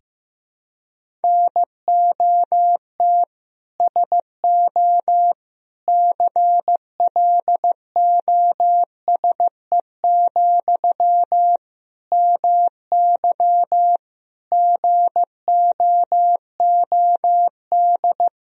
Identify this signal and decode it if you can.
Morse code, heard as a keyed tone, about 15 wpm, 700 Hz. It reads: NOT SO CLOSE, MY GOOD
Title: Sentences from Top 1000 Words - Review Part 1 - 15wpm